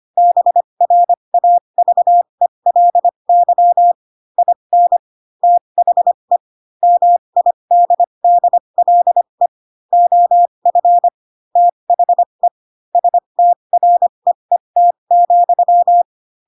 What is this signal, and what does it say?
Morse code, heard as a keyed tone, about 25 wpm, spaced as for 22 wpm, 700 Hz. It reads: BRAVELY IN THE MIDDLE OF THE STREET,
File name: Pinocchio / Pinocchio ch3 18 25 ik1hge